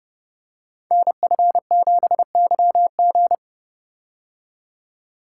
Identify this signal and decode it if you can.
Morse code, heard as a keyed tone, about 30 wpm, 700 Hz. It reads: NF7YG
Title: Code Groups - 5 Letter:Numbers - 30wpm